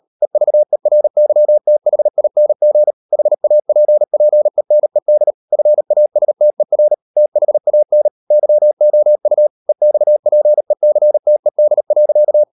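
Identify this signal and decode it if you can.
Morse code, heard as a keyed tone, about 38 wpm, 600 Hz. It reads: EVERYTHING HAPPENED FASTER THAN YOU EXPECTED.
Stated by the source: Morse code practice at 38 WPM, sending Ken Follett's novel Never in English, from CW med Gnister.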